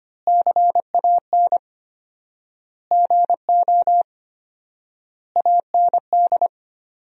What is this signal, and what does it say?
Morse code, heard as a keyed tone, about 25 wpm, 700 Hz. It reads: CAN GO AND
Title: 4x Word-Spacing - Sentences from Top 100 Words - 25wpm